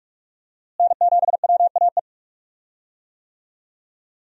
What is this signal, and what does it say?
Morse code, heard as a keyed tone, about 47 wpm, 700 Hz. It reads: N7WAE
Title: USA Call Signs - 5 Characters - Repeated 2x - 45wpm